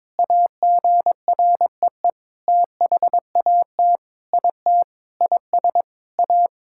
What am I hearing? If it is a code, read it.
Morse code, heard as a keyed tone, about 21 wpm, 700 Hz. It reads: AGREE THAT IT IS A
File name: Sentences with 7-Word Phrases - 22wpm